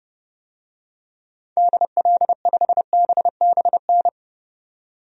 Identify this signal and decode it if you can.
Morse code, heard as a keyed tone, about 30 wpm, 700 Hz. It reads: DL5BBN
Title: DXCC-100 Call Signs - 6 Characters - 30wpm